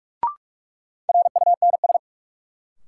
Morse code, heard as a keyed tone, about 41 wpm, 700 Hz. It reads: RUNS
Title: Sentences from Top 600 Words - New Word Drill - 45wpm